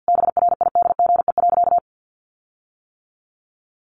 Morse code, heard as a keyed tone, about 50 wpm, 700 Hz. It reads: BRIDGE.
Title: Sentences from Top 1500 Words 50wpm